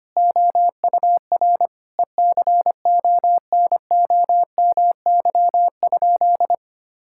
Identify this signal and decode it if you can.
Morse code, heard as a keyed tone, about 25 wpm, 700 Hz. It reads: OUR ECONOMY?
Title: Sentences from Top 1000 Words - Review Part 5 - 25wpm